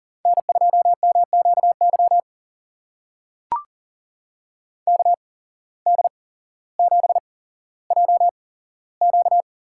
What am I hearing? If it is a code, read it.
Morse code, heard as a keyed tone, about 41 wpm, spaced as for 17 wpm, 700 Hz. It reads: N2MQY KD7JQ
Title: USA Call Signs - 5 Characters - 8x Character Spacing - 40wpm